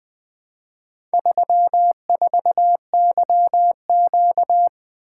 Morse code, heard as a keyed tone, about 21 wpm, 700 Hz. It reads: 34YQ
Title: Four Letter-Number Combinations - 20wpm